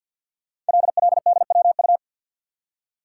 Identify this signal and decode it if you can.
Morse code, heard as a keyed tone, about 53 wpm, 700 Hz. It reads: LLDWU
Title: Code Groups - 5 Letter:Numbers - 50wpm